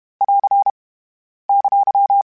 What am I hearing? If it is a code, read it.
Morse code, heard as a keyed tone, about 32 wpm, 800 Hz. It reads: <AR> !
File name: Q Calls / Q Calls-mp3s / Q CALLS SPSL @30